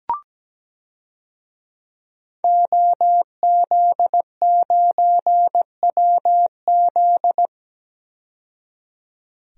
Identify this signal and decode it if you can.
Morse code, heard as a keyed tone, about 17 wpm, 700 Hz. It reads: OZ9WZ
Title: DXCC-100 Call Signs - 5 Characters - 17wpm